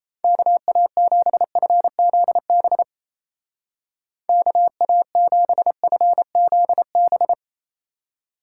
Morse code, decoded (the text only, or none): KA7FZB KA7FZB